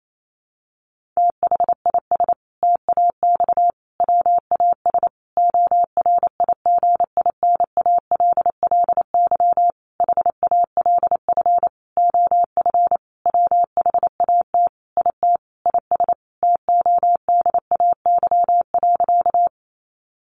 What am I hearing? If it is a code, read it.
Morse code, heard as a keyed tone, about 28 wpm, 700 Hz. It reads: THIS TAX WAS ORIGINALLY HALF OF WHAT IT IS TODAY.
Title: Sentences from Top 1500 Words 28wpm